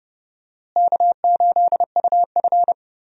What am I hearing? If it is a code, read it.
Morse code, heard as a keyed tone, about 30 wpm, 700 Hz. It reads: K8UF
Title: USA Call Signs - 4 Characters - Repeated 2x - 30wpm